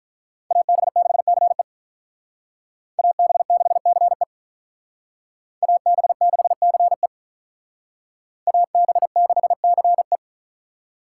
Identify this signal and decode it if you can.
Morse code, heard as a keyed tone, about 44 wpm, 700 Hz. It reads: AB6CE AB6CE AB6CE AB6CE
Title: Speed Racing - USA 2x2 Call Signs 35wpm